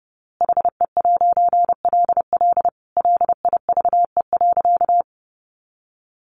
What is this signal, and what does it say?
Morse code, heard as a keyed tone, about 28 wpm, 700 Hz. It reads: HE'LL LIVE.
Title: A Sound Of Thunder 30wpm